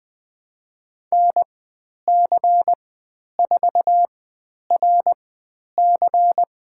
Morse code, heard as a keyed tone, about 20 wpm, spaced as for 10 wpm, 700 Hz. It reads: NC4RC
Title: 095 Lesson 16g - Letter c - Call Signs 20wpm